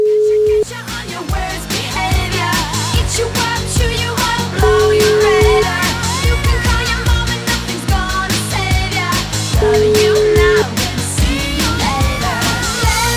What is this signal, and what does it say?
Ringback tone (Europe, 425 Hz), with noise.